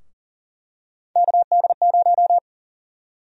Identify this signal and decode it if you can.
Morse code, heard as a keyed tone, about 40 wpm, 700 Hz. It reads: KD0
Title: USA Call Sign Prefixes - 40wpm